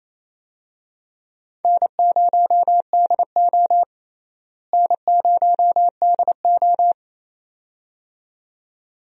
Morse code, heard as a keyed tone, about 28 wpm, 700 Hz. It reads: N0DO N0DO
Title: USA Call Signs - 4 Characters - Repeated 2x - 28wpm